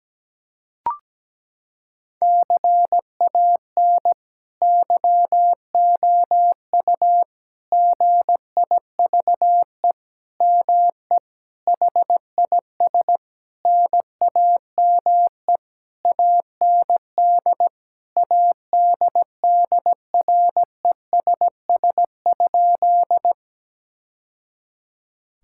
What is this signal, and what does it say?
Morse code, heard as a keyed tone, about 17 wpm, 700 Hz. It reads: CAN YOU GIVE ME HIS NAME AND ADDRESS?
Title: Sentences from Top 1000 Words - Review Part 6 - 17wpm